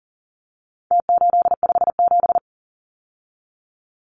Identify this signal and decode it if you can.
Morse code, heard as a keyed tone, about 40 wpm, 700 Hz. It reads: T857
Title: Numbers - Contest Style - 4-Digit 40wpm